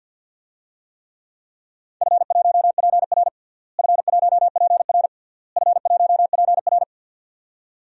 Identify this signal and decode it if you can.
Morse code, heard as a keyed tone, about 52 wpm, 700 Hz. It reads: F1PR F1PR F1PR
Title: DXCC-100 Call Signs - 4 Characters - Repeated 3x - 50wpm